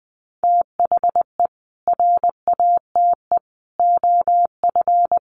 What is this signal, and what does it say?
Morse code, heard as a keyed tone, about 19 wpm, 700 Hz. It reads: THE RATE OF
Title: Sentences from Top 1100 Words 20wpm